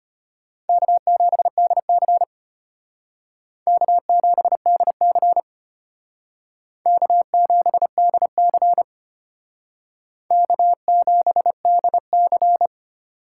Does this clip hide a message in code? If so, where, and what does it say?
Yes: Morse code from 0.7 to 12.7 s, heard as a keyed tone, about 33 wpm, 700 Hz. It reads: K7DC K7DC K7DC K7DC